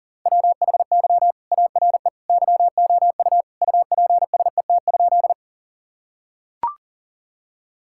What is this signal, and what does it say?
Morse code, heard as a keyed tone, about 40 wpm, 700 Hz. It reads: WHY ARE YOU UPSET?
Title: Sentences from Top 1000 Words - Review Part 5 - 40wpm